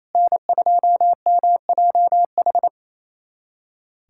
Morse code, heard as a keyed tone, about 28 wpm, 700 Hz. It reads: N2MJH
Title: USA Callsign DE Callsign - 5 Characters - 28wpm